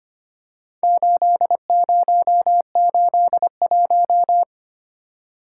Morse code, heard as a keyed tone, about 25 wpm, 690 Hz. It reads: 8081
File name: practice / 25wpm / Four Letter-Number Combinations - 25wpm